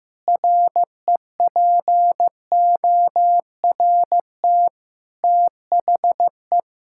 Morse code, heard as a keyed tone, about 15 wpm, 700 Hz. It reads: REPORT THE
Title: Sentences from Top 1000 Words - Review Part 6 - 15wpm